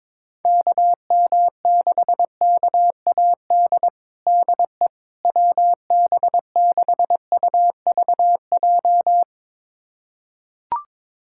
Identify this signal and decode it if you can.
Morse code, heard as a keyed tone, about 22 wpm, 700 Hz. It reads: KM6KAD DE WB6UVJ
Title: USA Callsign DE Callsign - 6 Characters - 22wpm